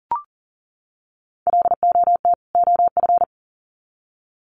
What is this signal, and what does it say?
Morse code, heard as a keyed tone, about 40 wpm, 700 Hz. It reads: LOT OF
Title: Sets of 2 Words 40wpm